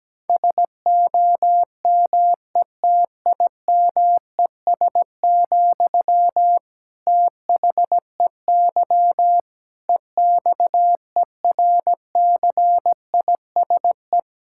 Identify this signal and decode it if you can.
Morse code, heard as a keyed tone, about 17 wpm, 700 Hz. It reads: SOMETIMES, THEY EXERCISE
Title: Sentences from Top 1000 Words - Review Part 6 - 17wpm